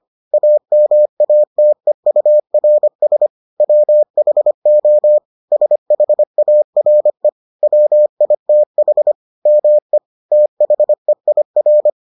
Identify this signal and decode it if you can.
Morse code, heard as a keyed tone, about 25 wpm, 600 Hz. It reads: AMATEURS WHO SHARE WITH ME THEIR